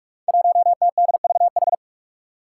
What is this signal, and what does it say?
Morse code, heard as a keyed tone, about 44 wpm, 700 Hz. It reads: 1TDVH
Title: Five Challenging Letter-Number Combinations 45wpm